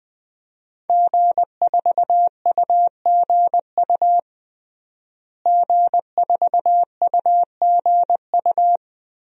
Morse code, heard as a keyed tone, about 20 wpm, 700 Hz. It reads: G4UGU G4UGU
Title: DXCC-100 Call Signs - 5 Characters - Repeated 2x - 20wpm